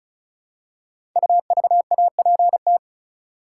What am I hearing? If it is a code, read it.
Morse code, heard as a keyed tone, about 34 wpm, 700 Hz. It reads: UVAPT
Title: Code Groups - 5 Letter:Numbers - 35wpm